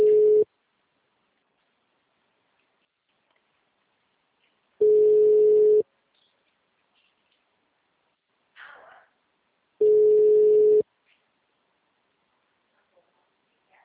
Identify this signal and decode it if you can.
Ringback tone (Europe, 425 Hz).